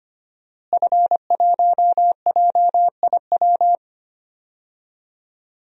Morse code, heard as a keyed tone, about 25 wpm, 700 Hz. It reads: F1JIW
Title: DXCC-100 Call Signs - 5 Characters - Repeated 3x - 25wpm